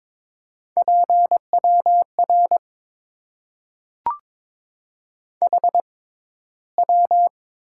Morse code, heard as a keyed tone, about 22 wpm, spaced as for 14 wpm, 700 Hz. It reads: PWR HW
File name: Top 90 QSO Elements - 6x Character Spacing - 22wpm